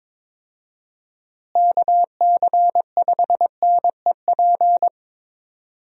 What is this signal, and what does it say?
Morse code, heard as a keyed tone, about 22 wpm, 700 Hz. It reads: KC5NEP
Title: USA Call Signs - 6 Characters - Repeated 2x - 22wpm